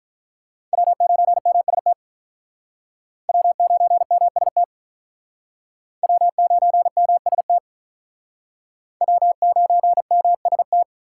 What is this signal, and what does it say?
Morse code, heard as a keyed tone, about 43 wpm, 700 Hz. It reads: W9MST W9MST W9MST W9MST
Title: Speed Racing - USA 1x3 Call Signs 35wpm